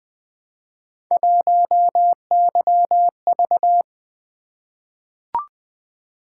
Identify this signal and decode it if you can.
Morse code, heard as a keyed tone, about 20 wpm, 700 Hz. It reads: 1YV
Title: Three Letter-Number Combinations - 20wpm